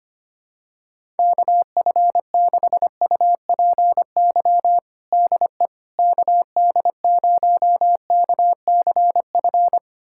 Morse code, heard as a keyed tone, about 25 wpm, 700 Hz. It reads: KF6UPY DE KD0KCF
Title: USA Callsign DE Callsign - 6 Characters - 25wpm